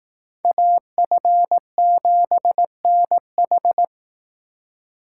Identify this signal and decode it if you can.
Morse code, heard as a keyed tone, about 18 wpm, 700 Hz. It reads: AF7NH
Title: Speed Racing - USA 2x2 Call Signs 15wpm